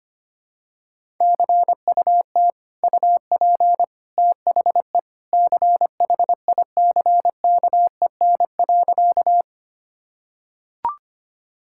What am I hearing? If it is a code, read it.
Morse code, heard as a keyed tone, about 25 wpm, 700 Hz. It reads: CUT UP THE CHICKEN.
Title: Sentences from Top 1000 Words - Review Part 5 - 25wpm